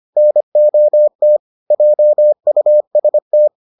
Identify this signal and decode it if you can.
Morse code, heard as a keyed tone, about 25 wpm, 600 Hz. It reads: NOT JUST